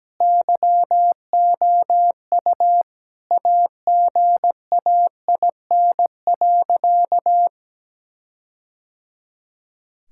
Morse code, heard as a keyed tone, about 17 wpm, 700 Hz. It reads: YOU AGAIN.
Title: Sentences from Top 1600 Words 17wpm